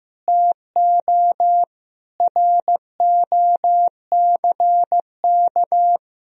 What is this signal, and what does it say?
Morse code, heard as a keyed tone, about 15 wpm, 700 Hz. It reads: TO ROCK
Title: Sentences from Top 1000 Words - Review Part 3 - 15wpm